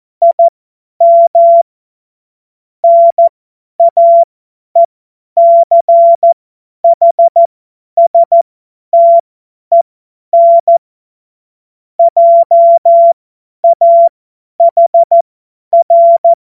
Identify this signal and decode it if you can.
Morse code, heard as a keyed tone, about 14 wpm, spaced as for 10 wpm, 680 Hz. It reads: IM NAECHSTEN JAHR
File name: Morsekurs / Witze / Witz 10. Januar 2023 50 F70